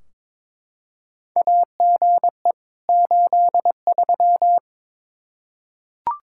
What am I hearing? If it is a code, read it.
Morse code, heard as a keyed tone, about 22 wpm, 700 Hz. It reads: AGE 83
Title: Practice QSO - Exchange 3 - More Info - 22wpm